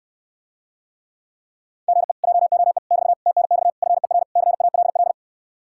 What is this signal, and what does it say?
Morse code, heard as a keyed tone, about 69 wpm, 700 Hz. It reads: GE JOE <BT> TNX FER CALL